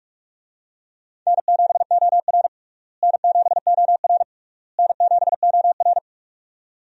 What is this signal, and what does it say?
Morse code, heard as a keyed tone, about 46 wpm, 700 Hz. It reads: N7OR N7OR N7OR
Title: USA Call Signs - 4 Characters - Repeated 3x - 45wpm